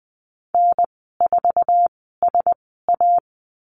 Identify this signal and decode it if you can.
Morse code, heard as a keyed tone, about 20 wpm, spaced as for 15 wpm, 700 Hz. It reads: N4SA
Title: 062 Lesson 11g - Letter h - Call Signs 20wpm